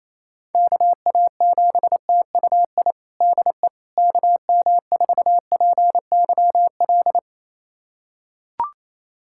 Morse code, heard as a keyed tone, about 28 wpm, 700 Hz. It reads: KA7TUI DE KM4PYL
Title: USA Callsign DE Callsign 28wpm